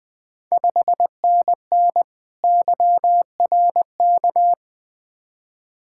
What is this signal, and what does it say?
Morse code, heard as a keyed tone, about 20 wpm, 690 Hz. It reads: 5NN YRK
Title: Ontario QSO Party Exchange 20wpm